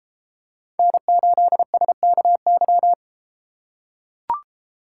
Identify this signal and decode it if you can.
Morse code, heard as a keyed tone, about 33 wpm, 700 Hz. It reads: N8SKY